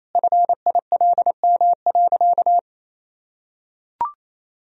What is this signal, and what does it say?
Morse code, heard as a keyed tone, about 28 wpm, 700 Hz. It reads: FILM.
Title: Sentences from Top 1000 Words - Review Part 3 - 28wpm